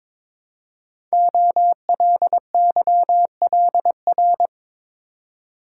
Morse code, heard as a keyed tone, about 22 wpm, 700 Hz. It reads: OLYLR
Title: Code Groups - 5 Letter:Numbers - 22wpm